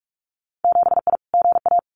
Morse code, heard as a keyed tone, about 46 wpm, 700 Hz. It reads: 7I GA
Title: Winter Field Day Exchanges Encore 45wpm